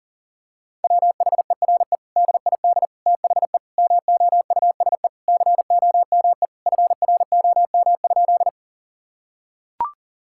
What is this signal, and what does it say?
Morse code, heard as a keyed tone, about 40 wpm, 700 Hz. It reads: WHERE DID THE MOUSE COME FROM?
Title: Sentences from Top 1000 Words - Review Part 5 - 40wpm